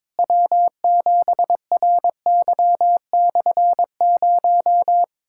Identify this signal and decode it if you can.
Morse code, heard as a keyed tone, about 22 wpm, 700 Hz. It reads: W7RY/0
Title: K1USN SST QSO Practice - 22wpm